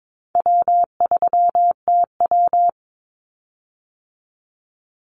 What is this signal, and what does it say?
Morse code, heard as a keyed tone, about 22 wpm, 700 Hz. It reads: W3TW
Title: Call Signs - 22wpm